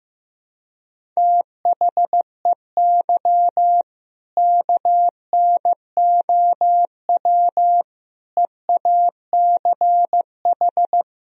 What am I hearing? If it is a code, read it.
Morse code, heard as a keyed tone, about 15 wpm, 700 Hz. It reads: THEY KNOW EACH
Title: Sentences from Top 1000 Words - Review Part 1 - 15wpm